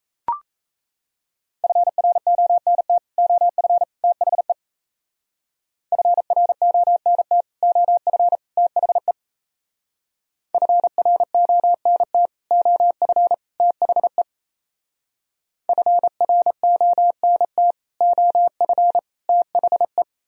Morse code, heard as a keyed tone, about 36 wpm, 700 Hz. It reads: FRONT OF THE FRONT OF THE FRONT OF THE FRONT OF THE